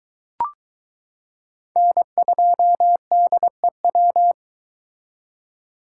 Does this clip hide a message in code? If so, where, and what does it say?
Yes: Morse code from 1.8 to 4.3 s, heard as a keyed tone, about 23 wpm, 700 Hz. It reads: N2DEW